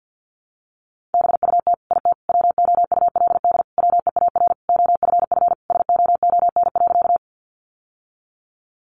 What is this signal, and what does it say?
Morse code, heard as a keyed tone, about 50 wpm, 700 Hz. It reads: BUT IT WOULD WEAR OFF SOON.
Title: Sentences from Top 1500 Words 50wpm